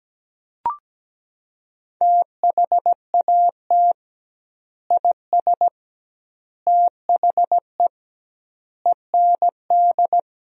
Morse code, heard as a keyed tone, about 17 wpm, 700 Hz. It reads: THAT IS THE END
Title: 2x Word-Spacing - Sentences from Top 200 Words - 17wpm